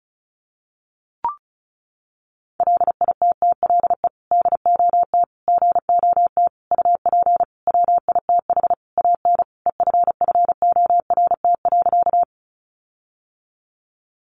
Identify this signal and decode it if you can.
Morse code, heard as a keyed tone, about 34 wpm, 700 Hz. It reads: LITTLE DOT GOT UP WITH AN EFFORT.
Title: Sentences from Top 1500 Words 35wpm